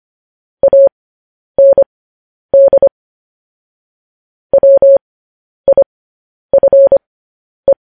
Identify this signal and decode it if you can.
Morse code, heard as a keyed tone, about 25 wpm, spaced as for 10 wpm, 550 Hz. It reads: AND WIFE
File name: podcast / News headlines-1x-25x10